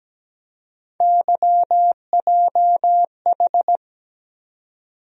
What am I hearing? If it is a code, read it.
Morse code, heard as a keyed tone, about 17 wpm, 700 Hz. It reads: YJH
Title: USA Call Sign Postfixes - 3 Letters 17wpm